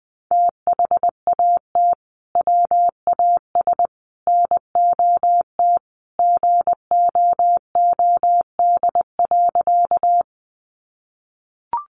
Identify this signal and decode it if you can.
Morse code, heard as a keyed tone, about 20 wpm, 700 Hz. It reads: THAT WAS NOT GOOD.
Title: Sentences from Top 600 Words 20wpm